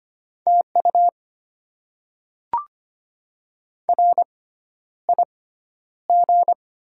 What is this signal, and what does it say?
Morse code, heard as a keyed tone, about 25 wpm, spaced as for 11 wpm, 700 Hz. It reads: TU RIG
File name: Top 20 QSO Elements - 6x Character Spacing - 25wpm